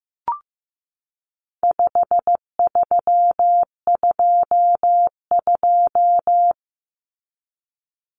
Morse code, heard as a keyed tone, about 15 wpm, 700 Hz. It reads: 5322
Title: Common Cut Numbers - 4-Digit 15wpm